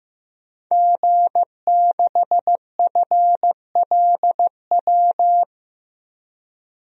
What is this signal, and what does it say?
Morse code, heard as a keyed tone, about 15 wpm, 700 Hz. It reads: G6FLW
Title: DXCC-100 Call Signs - 5 Characters - 15wpm